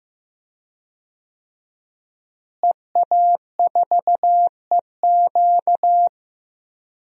Morse code, heard as a keyed tone, about 16 wpm, 700 Hz. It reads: EA4EQ
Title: DXCC-100 Call Signs - 5 Characters - Repeated 3x - 15wpm